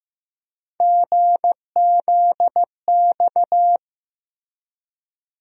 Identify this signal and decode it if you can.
Morse code, heard as a keyed tone, about 15 wpm, 700 Hz. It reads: GZX
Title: USA Call Sign Postfixes - 3 Letters 15wpm